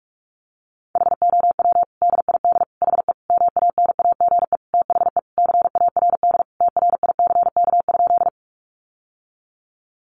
Morse code, heard as a keyed tone, about 44 wpm, 700 Hz. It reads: HOW DID HE MANAGE THE CARD TRICK?
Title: Sentences from Top 1500 Words 45wpm